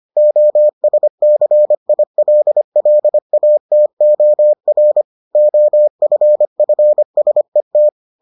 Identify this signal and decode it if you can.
Morse code, heard as a keyed tone, about 25 wpm, 600 Hz. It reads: OSCILLATOR OFFSET